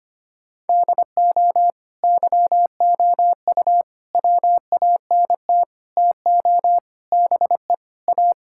Morse code, heard as a keyed tone, about 25 wpm, 700 Hz. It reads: DO YOU WANT TO BE A
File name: practice / 25wpm / Sentences with 4-Word Phrases - 25wpm